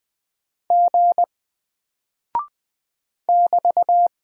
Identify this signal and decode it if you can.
Morse code, heard as a keyed tone, about 20 wpm, 700 Hz. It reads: G <BT>
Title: Single Character (Letter:Number:Punctuation:Prosign) Alt 20wpm